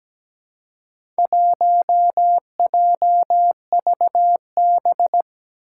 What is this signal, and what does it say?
Morse code, heard as a keyed tone, about 17 wpm, 700 Hz. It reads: 1JVB
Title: Four Challenging Letter-Number Combinations 17wpm